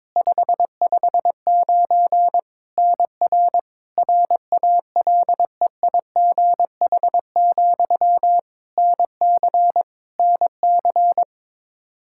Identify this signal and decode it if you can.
Morse code, heard as a keyed tone, about 21 wpm, 700 Hz. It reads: 559 NR RALEIGH, NC NC
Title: K1USN SST QSO Practice 22wpm